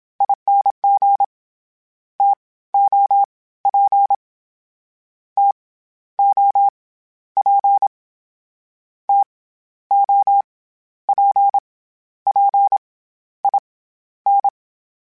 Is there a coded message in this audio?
Yes: Morse code from 0.2 to 14.5 s, heard as a keyed tone, about 27 wpm, 800 Hz. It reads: ING T O P T O P T O P P I N